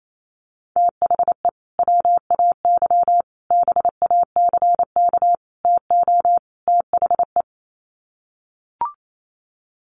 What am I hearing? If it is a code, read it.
Morse code, heard as a keyed tone, about 28 wpm, 700 Hz. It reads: THE WAY BACK TO THE
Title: Sets of 5 Words Encore2 28wpm